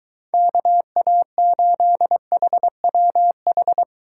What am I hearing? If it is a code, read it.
Morse code, heard as a keyed tone, about 23 wpm, 700 Hz. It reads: KA8HWH